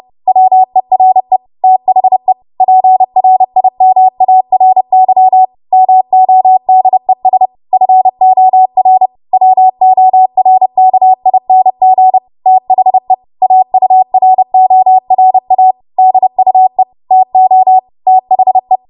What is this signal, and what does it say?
Morse code, heard as a keyed tone, about 30 wpm, 750 Hz. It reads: WERE THE PRIMARY MODES FOR WORKING THE AURORA DUE TO THE
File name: Morse / 30WPM